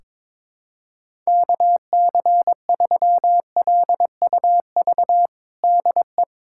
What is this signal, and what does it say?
Morse code, heard as a keyed tone, about 21 wpm, 700 Hz. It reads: KC3LUV DE
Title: USA Callsign DE Callsign 22wpm